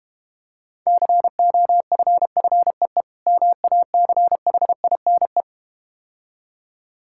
Morse code, heard as a keyed tone, about 31 wpm, 700 Hz. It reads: COFFEE MACHINE
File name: practice / 25wpm / Difficult Words - Speed-Racing with Hints - 25wpm